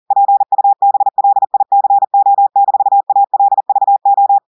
Morse code, heard as a keyed tone, about 40 wpm, 800 Hz. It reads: PUBLICO-ALVO
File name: audios / portugais / portugais texte en clair 2 vitesse 40 mots minutes